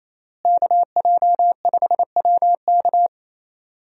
Morse code, heard as a keyed tone, about 28 wpm, 700 Hz. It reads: KJ5WK
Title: USA Call Signs - 5 Characters - Repeated 2x - 28wpm